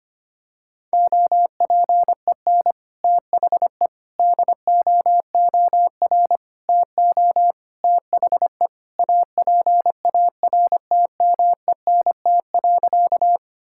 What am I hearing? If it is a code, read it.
Morse code, heard as a keyed tone, about 25 wpm, 700 Hz. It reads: OPEN THE DOOR TO THE APARTMENT.
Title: Sentences from Top 1000 Words - Review Part 6 - 25wpm